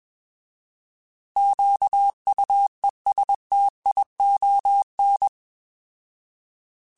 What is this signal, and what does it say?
Morse code, heard as a keyed tone, about 21 wpm, 800 Hz. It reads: QUESTION
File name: Alphabetical / Alphabetical mp3s / Qs @20